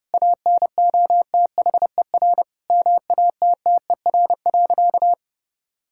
Morse code, heard as a keyed tone, about 32 wpm, 700 Hz. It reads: ANOTHER MATTER.
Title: Sentences from Top 1000 Words - Review Part 5 - 30wpm